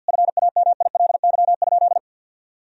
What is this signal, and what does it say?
Morse code, heard as a keyed tone, about 50 wpm, 700 Hz. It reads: FAMILY?
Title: Sentences from Top 1000 Words - Review Part 4 - 50wpm